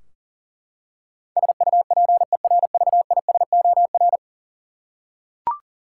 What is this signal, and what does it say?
Morse code, heard as a keyed tone, about 40 wpm, 700 Hz. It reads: SUPERVISOR
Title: Practice QSO - Exchange 3 - More Info - 40wpm